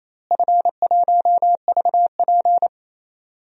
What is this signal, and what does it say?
Morse code, heard as a keyed tone, about 28 wpm, 700 Hz. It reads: F1VP
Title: DXCC-100 Call Signs - 4 Characters - Repeated 3x - 28wpm